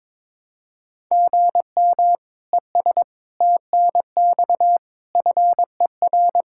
Morse code, heard as a keyed tone, about 22 wpm, 700 Hz. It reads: GM ES TNX FER
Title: K1USN SST QSO Practice 22wpm